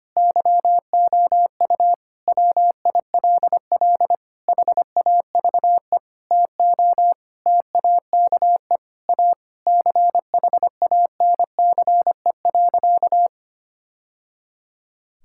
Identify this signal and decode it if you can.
Morse code, heard as a keyed tone, about 25 wpm, 700 Hz. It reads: YOU WILL HAVE TO TAKE A CHANCE.
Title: Sentences from Top 1000 Words - Review Part 6 - 25wpm